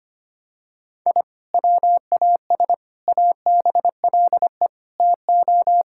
Morse code, heard as a keyed tone, about 25 wpm, 700 Hz. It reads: I WAS ABLE TO
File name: practice / 25wpm / Sentences with 4-Word Phrases - 25wpm